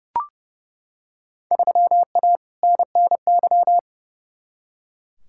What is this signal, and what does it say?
Morse code, heard as a keyed tone, about 32 wpm, 700 Hz. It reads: 3A NNY
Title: Field Day Exchanges - 30wpm